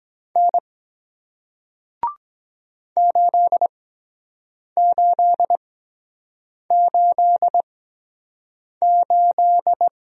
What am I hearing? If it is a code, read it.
Morse code, heard as a keyed tone, about 24 wpm, spaced as for 7 wpm, 700 Hz. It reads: N 8888